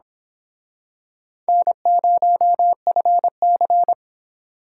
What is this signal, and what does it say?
Morse code, heard as a keyed tone, about 25 wpm, 700 Hz. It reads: N0FC
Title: Speed Racing - USA 1x2 Call Signs 22wpm